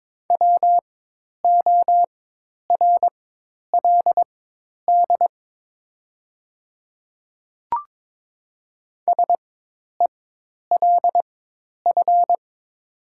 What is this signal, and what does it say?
Morse code, heard as a keyed tone, about 22 wpm, spaced as for 10 wpm, 700 Hz. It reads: WORLD SELF
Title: 4x Character Spacing - Top 300 Words - 22wpm